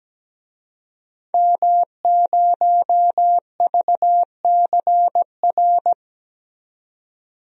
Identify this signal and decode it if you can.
Morse code, heard as a keyed tone, about 17 wpm, 700 Hz. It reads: M0VCR
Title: DXCC-100 Call Signs - 5 Characters - 17wpm